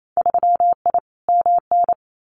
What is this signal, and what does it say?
Morse code, heard as a keyed tone, about 28 wpm, 700 Hz. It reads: 3I MN
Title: Winter Field Day Exchanges 28wpm